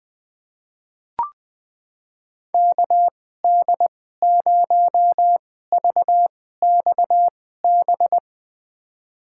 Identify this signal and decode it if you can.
Morse code, heard as a keyed tone, about 20 wpm, spaced as for 15 wpm, 700 Hz. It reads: KD0VXB
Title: USA Call Signs - 6 Characters - 2x Character Spacing - 20wpm